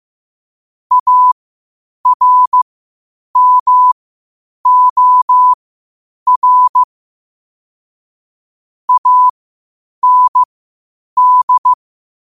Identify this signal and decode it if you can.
Morse code, heard as a keyed tone, about 15 wpm, spaced as for 8 wpm, 1000 Hz. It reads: ARMOR AND
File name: qotdfiles / 8wpm